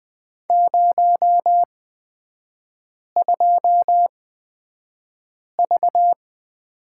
Morse code, heard as a keyed tone, about 20 wpm, spaced as for 5 wpm, 700 Hz. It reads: 02V
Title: 250 Lesson 40e - BK - Three Characters 20wpm